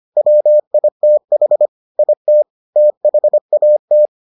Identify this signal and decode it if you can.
Morse code, heard as a keyed tone, about 25 wpm, 600 Hz. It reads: WITH IT THAT